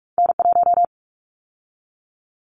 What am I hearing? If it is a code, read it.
Morse code, heard as a keyed tone, about 46 wpm, 700 Hz. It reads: N1
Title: Callsign Prefixes for Top 25 DXCC - 45wpm